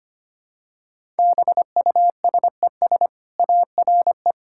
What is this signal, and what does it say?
Morse code, heard as a keyed tone, about 25 wpm, 700 Hz. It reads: BUSES ARE
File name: Easy Sentences with Hints - 25wpm